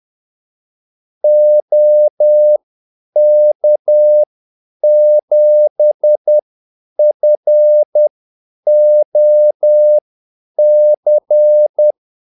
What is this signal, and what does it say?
Morse code, heard as a keyed tone, about 10 wpm, spaced as for 8 wpm, 600 Hz. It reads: OK7FOC
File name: Callsigns-10-8wpm